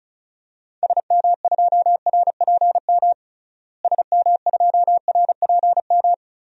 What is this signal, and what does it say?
Morse code, heard as a keyed tone, about 36 wpm, 700 Hz. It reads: SM2RPM SM2RPM
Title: DXCC-100 Call Signs - 6 Characters - Repeated 3x - 35wpm